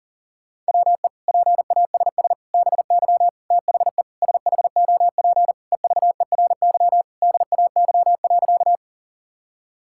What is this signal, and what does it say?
Morse code, heard as a keyed tone, about 37 wpm, 700 Hz. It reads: WE PASS BY THE SHOP EVERY DAY.
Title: Sentences from Top 1600 Words 40wpm